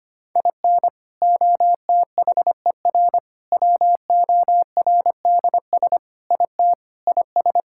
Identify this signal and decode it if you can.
Morse code, heard as a keyed tone, about 25 wpm, 700 Hz. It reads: IN OTHER WORDS IT IS
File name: practice / 25wpm / Sentences with 3-Word Phrases - 25wpm